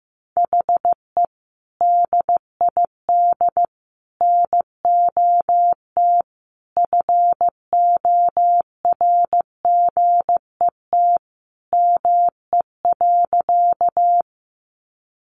Morse code, heard as a keyed tone, about 15 wpm, 700 Hz. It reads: HE DID NOT FORGET ME.
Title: Sentences from Top 1200 Words 15wpm